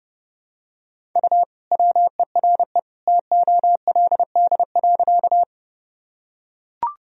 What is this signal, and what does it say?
Morse code, heard as a keyed tone, about 32 wpm, 700 Hz. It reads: U WERE TOLD.
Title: Sentences from Top 400 Words and Abbreviations 30wpm